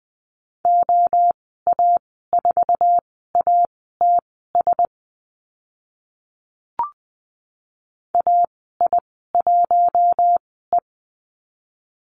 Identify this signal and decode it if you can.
Morse code, heard as a keyed tone, about 20 wpm, spaced as for 14 wpm, 700 Hz. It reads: OA4ATS AI1E
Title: 062 Lesson 11g - Letter h - Call Signs 20wpm